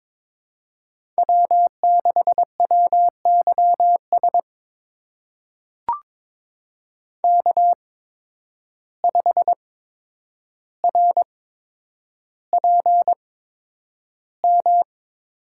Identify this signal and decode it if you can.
Morse code, heard as a keyed tone, about 22 wpm, spaced as for 9 wpm, 700 Hz. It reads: W6WYS K5RPM
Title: USA Call Signs - 5 Characters - 8x Character Spacing - 22wpm